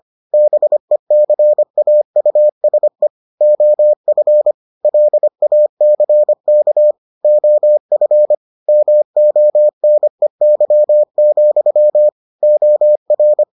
Morse code, heard as a keyed tone, about 25 wpm, 600 Hz. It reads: BECAUSE OF LACK OF MONEY, OR